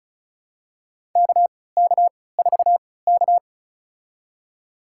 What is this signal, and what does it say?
Morse code, heard as a keyed tone, about 36 wpm, spaced as for 20 wpm, 700 Hz. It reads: KK4K
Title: USA Call Signs - 4 Characters - 3x Character Spacing - 35wpm